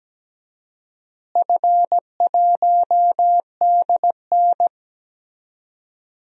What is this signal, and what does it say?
Morse code, heard as a keyed tone, about 17 wpm, 700 Hz. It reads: F1DN
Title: DXCC-100 Call Signs - 4 Characters - Repeated 3x - 17wpm